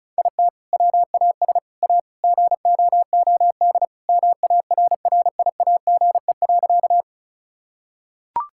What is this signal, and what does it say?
Morse code, heard as a keyed tone, about 36 wpm, 700 Hz. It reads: IT WAS A GOOD MARRIAGE.
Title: Sentences from Top 1000 Words - Review Part 5 - 35wpm